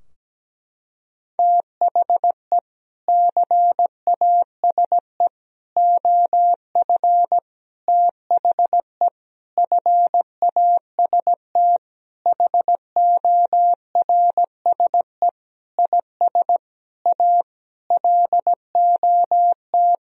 Morse code, heard as a keyed tone, about 17 wpm, 700 Hz. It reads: THE CASE OF THE FAST HORSE IS A LOT
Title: Sentences from Top 1000 Words - Review Part 2 - 17wpm